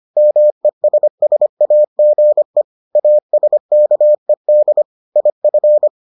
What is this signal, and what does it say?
Morse code, heard as a keyed tone, about 25 wpm, 600 Hz. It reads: MESSAGE ASKED IF